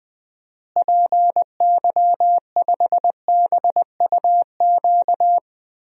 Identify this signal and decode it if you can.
Morse code, heard as a keyed tone, about 20 wpm, 700 Hz. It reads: PY5BUQ
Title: DXCC-100 Call Signs - 6 Characters - Repeated 3x - 20wpm